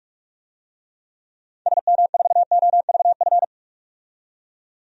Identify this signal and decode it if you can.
Morse code, heard as a keyed tone, about 42 wpm, 700 Hz. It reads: SM4OVF
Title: DXCC-100 Call Signs - 6 Characters - 45wpm